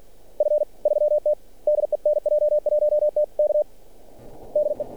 Morse code, heard as a keyed tone, about 46 wpm, 600 Hz. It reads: F3T DE NJ1T K DE
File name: archived / CRSnet